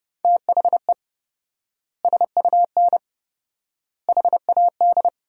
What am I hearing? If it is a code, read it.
Morse code, heard as a keyed tone, about 32 wpm, 700 Hz. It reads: THE SUN HAD
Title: 4x Word-Spacing - Sentences from Top 300 Words - 30wpm